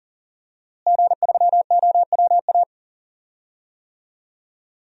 Morse code, heard as a keyed tone, about 40 wpm, 700 Hz. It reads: G3OWA
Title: DXCC-100 Call Signs - 5 Characters - Repeated 3x - 40wpm